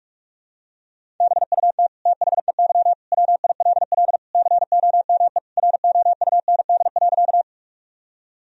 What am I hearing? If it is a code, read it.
Morse code, heard as a keyed tone, about 44 wpm, 700 Hz. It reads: BUT THEY WILL COME ROUND.
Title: Sentences from Top 1000 Words - Review Part 1 - 45wpm